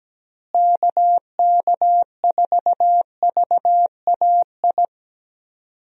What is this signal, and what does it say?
Morse code, heard as a keyed tone, about 17 wpm, 700 Hz. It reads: KK4VAI
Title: USA Call Signs - 6 Characters - Repeated 2x - 17wpm